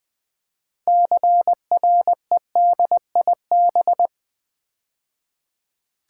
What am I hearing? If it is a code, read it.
Morse code, heard as a keyed tone, about 20 wpm, 700 Hz. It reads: CREDIB
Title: Build A Word - 2 Letters At A Time - 8-Letter Words - 20wpm